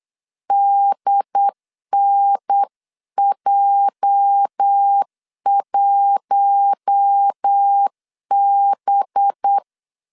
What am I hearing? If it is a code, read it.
Morse code, heard as a keyed tone, about 8 wpm, 800 Hz. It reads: DNJ1B
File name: CW / DET / 1 9 1500C S C 8wpm 5653